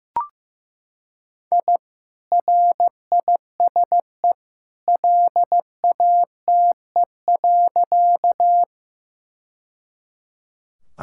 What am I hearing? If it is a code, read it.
Morse code, heard as a keyed tone, about 15 wpm, 700 Hz. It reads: I RISE LATE.
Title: Sentences from Top 1000 Words - Review Part 6 - 15wpm